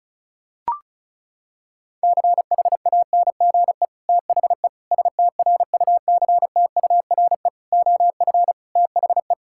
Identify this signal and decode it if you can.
Morse code, heard as a keyed tone, about 36 wpm, 700 Hz. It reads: CHANGE THE STRUCTURE OF THE
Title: Sentences from Top 1000 Words - Review Part 6 - 35wpm